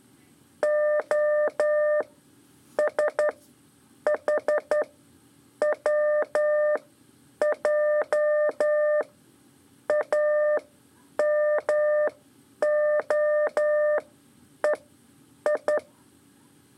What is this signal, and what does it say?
Morse code, heard as a keyed tone, about 11 wpm, spaced as for 8 wpm, 600 Hz. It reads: OSHWJAMOEI
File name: 3 lezione CW 2 AIR